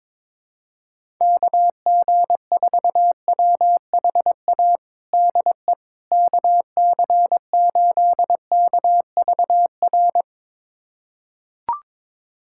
Morse code, heard as a keyed tone, about 22 wpm, 700 Hz. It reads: KG4WHA DE KC8KVR
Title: USA Callsign DE Callsign - 6 Characters - 22wpm